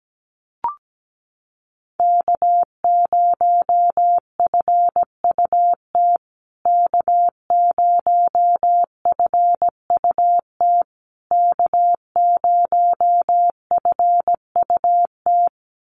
Morse code, heard as a keyed tone, about 17 wpm, 700 Hz. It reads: K0FUT K0FUT K0FUT
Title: USA Call Signs Repeated 3x 17wpm